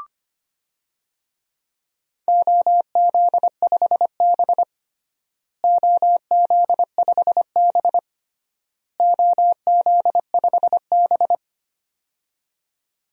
Morse code, heard as a keyed tone, about 25 wpm, 700 Hz. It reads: OZ5B OZ5B OZ5B
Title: DXCC-100 Call Signs - 4 Characters - Repeated 3x - 25wpm